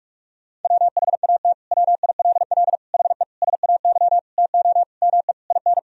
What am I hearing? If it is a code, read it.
Morse code, heard as a keyed tone, about 41 wpm, 700 Hz. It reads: WHAT WILL HE SAY TO ME IN
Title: Sentences from Top 1000 Words - Review Part 3 - 45wpm